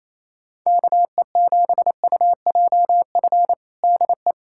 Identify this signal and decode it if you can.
Morse code, heard as a keyed tone, about 28 wpm, 700 Hz. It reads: KE7UJF DE
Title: USA Callsign DE Callsign - 6 Characters - 28wpm